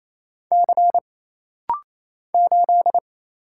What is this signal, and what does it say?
Morse code, heard as a keyed tone, about 28 wpm, 700 Hz. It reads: C 8
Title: Single Letter-Number Alt - Rapid-Fire 28wpm